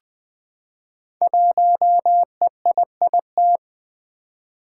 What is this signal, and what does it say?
Morse code, heard as a keyed tone, about 20 wpm, 700 Hz. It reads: 1EIIT
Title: Five Challenging Letter-Number Combinations 20wpm